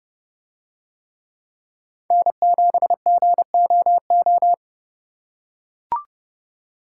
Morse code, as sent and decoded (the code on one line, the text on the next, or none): -. --... --. --- ---
N7GOO